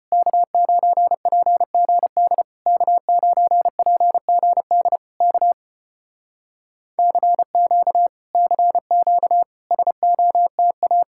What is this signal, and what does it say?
Morse code, heard as a keyed tone, about 34 wpm, 700 Hz. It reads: K9PGD K9PGD K CQ CQ SOTA